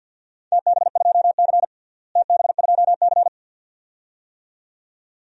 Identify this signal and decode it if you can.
Morse code, heard as a keyed tone, about 53 wpm, 700 Hz. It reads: TB2C TB2C
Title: DXCC-100 Call Signs - 4 Characters - Repeated 2x - 50wpm